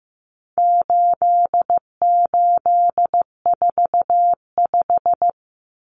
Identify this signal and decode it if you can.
Morse code, heard as a keyed tone, about 15 wpm, 700 Hz. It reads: 8845
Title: Numbers - 4-Digit 15wpm